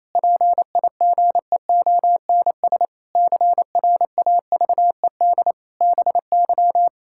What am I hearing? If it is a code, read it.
Morse code, heard as a keyed tone, about 28 wpm, 700 Hz. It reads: PIGEONS CRAVED BY
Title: Pangram Challenge v2 - 28wpm